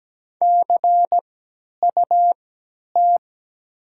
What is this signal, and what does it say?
Morse code, heard as a keyed tone, about 17 wpm, spaced as for 10 wpm, 700 Hz. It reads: CUT
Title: 3x Character Spacing - Top 300 Words - 17wpm